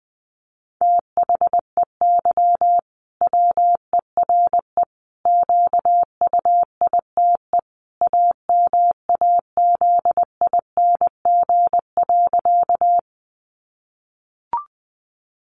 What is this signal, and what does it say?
Morse code, heard as a keyed tone, about 20 wpm, 700 Hz. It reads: THEY WERE QUITE AMAZING.
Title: Sentences from Top 1500 Words 20wpm